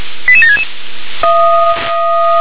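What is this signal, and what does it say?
Five-tone selective call (ZVEI), heard as quick runs of five tones, tones of about 70 ms. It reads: 88766
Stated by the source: ZVEI five-tone call received on an SDR.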